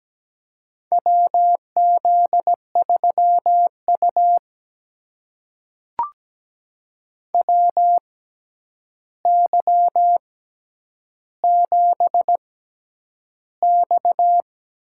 Morse code, heard as a keyed tone, about 17 wpm, spaced as for 9 wpm, 700 Hz. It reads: WZ3U WY7X
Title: USA Call Signs - 4 Characters - 6x Character Spacing - 17wpm